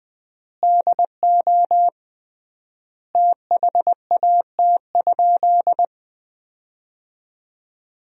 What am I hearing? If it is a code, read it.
Morse code, heard as a keyed tone, about 20 wpm, 700 Hz. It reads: DO THAT?
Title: 3x Word-Spacing - Sentences from Top 100 Words - 20wpm